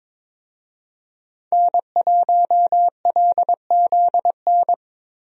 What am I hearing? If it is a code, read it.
Morse code, heard as a keyed tone, about 22 wpm, 700 Hz. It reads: N1LZN
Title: USA Call Signs - 5 Characters - Repeated 3x - 22wpm